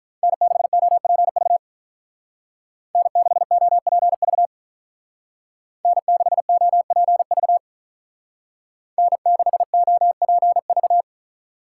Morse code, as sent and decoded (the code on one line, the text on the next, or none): -. -.... --- .--. ...- / -. -.... --- .--. ...- / -. -.... --- .--. ...- / -. -.... --- .--. ...-
N6OPV N6OPV N6OPV N6OPV